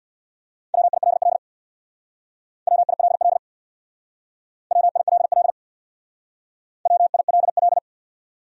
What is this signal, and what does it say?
Morse code, heard as a keyed tone, about 64 wpm, 700 Hz. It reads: WILL WILL WILL WILL